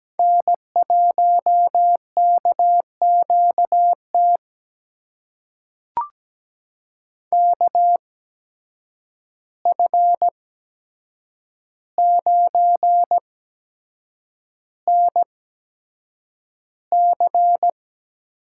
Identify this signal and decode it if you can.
Morse code, heard as a keyed tone, about 17 wpm, spaced as for 7 wpm, 700 Hz. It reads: N1KQT KF9NC
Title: USA Call Signs - 5 Characters - 8x Character Spacing - 17wpm